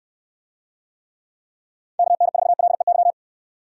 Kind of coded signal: Morse code

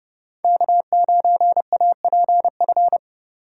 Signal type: Morse code

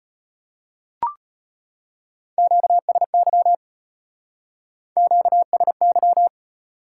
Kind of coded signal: Morse code

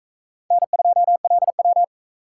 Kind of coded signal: Morse code

